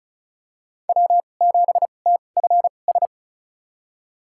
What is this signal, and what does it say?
Morse code, heard as a keyed tone, about 36 wpm, spaced as for 25 wpm, 700 Hz. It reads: W7TFS
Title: USA Call Signs - 5 Characters - 2x Character Spacing - 35wpm